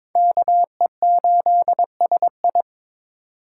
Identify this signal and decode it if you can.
Morse code, heard as a keyed tone, about 26 wpm, 700 Hz. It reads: KE8SI
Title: Speed Racing - USA 2x2 Call Signs 22wpm